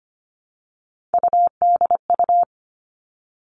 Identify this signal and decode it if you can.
Morse code, heard as a keyed tone, about 25 wpm, 700 Hz. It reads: UDU
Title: Three Challenging Letter-Number Combinations 25wpm